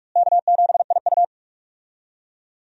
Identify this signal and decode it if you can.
Morse code, heard as a keyed tone, about 46 wpm, 700 Hz. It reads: K7IU
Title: USA Call Signs - 4 Characters - Repeated 2x - 45wpm